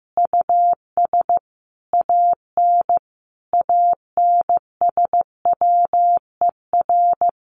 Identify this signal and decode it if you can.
Morse code, heard as a keyed tone, about 14 wpm, 700 Hz. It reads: US AN ANSWER
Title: Sentences from Top 600 Words 15wpm